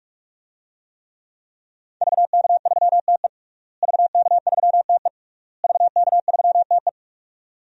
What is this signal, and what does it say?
Morse code, heard as a keyed tone, about 43 wpm, 700 Hz. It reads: VK3TE VK3TE VK3TE
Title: DXCC-100 Call Signs - 5 Characters - Repeated 3x - 45wpm